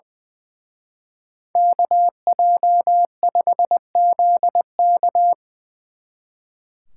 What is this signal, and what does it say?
Morse code, heard as a keyed tone, about 23 wpm, 700 Hz. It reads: KJ5ZK
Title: Speed Racing - USA 2x2 Call Signs 20wpm